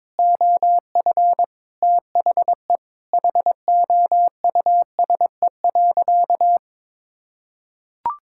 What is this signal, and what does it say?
Morse code, heard as a keyed tone, about 22 wpm, 700 Hz. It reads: OF THE HOUSE.
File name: practice / 22wpm / Sentences with 4-Word Phrases - 22wpm